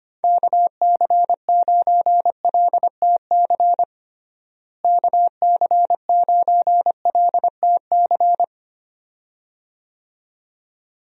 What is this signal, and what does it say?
Morse code, heard as a keyed tone, about 25 wpm, 700 Hz. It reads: KC9LTC KC9LTC
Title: USA Call Signs - 6 Characters - Repeated 2x - 25wpm